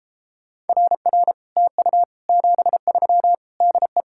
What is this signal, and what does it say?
Morse code, heard as a keyed tone, about 33 wpm, 700 Hz. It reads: RR TU 73 DE